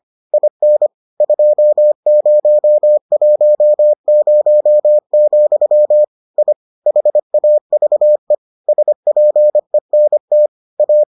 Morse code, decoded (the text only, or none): IN 2010, I HAVE SPENT A